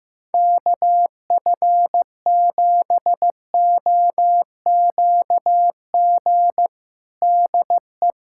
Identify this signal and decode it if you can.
Morse code, heard as a keyed tone, about 15 wpm, 700 Hz. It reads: KF7OQG DE
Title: USA Callsign DE Callsign - 6 Characters - 15wpm